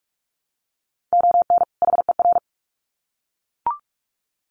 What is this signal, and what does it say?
Morse code, heard as a keyed tone, about 44 wpm, 700 Hz. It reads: ON HER
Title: Sets of 2 Words 45wpm